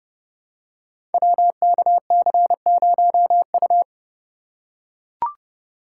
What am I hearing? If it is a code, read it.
Morse code, heard as a keyed tone, about 30 wpm, 700 Hz. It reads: WKC0U
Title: Code Groups - 5 Letter:Numbers - 30wpm